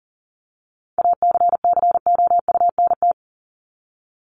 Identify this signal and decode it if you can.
Morse code, heard as a keyed tone, about 40 wpm, 700 Hz. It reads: ACCOUNT
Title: Top 751-1000 Words 40wpm